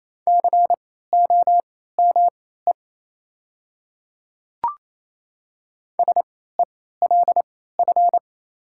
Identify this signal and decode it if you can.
Morse code, heard as a keyed tone, about 28 wpm, spaced as for 16 wpm, 700 Hz. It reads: COME SELF
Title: 3x Character Spacing - Top 300 Words - 28wpm